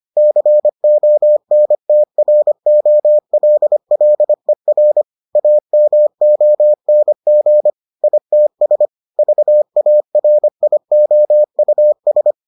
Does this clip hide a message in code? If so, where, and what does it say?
Yes: Morse code from 0.2 to 12.3 s, heard as a keyed tone, about 25 wpm, 600 Hz. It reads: CONTROLLER AMONG ITS VARIOUS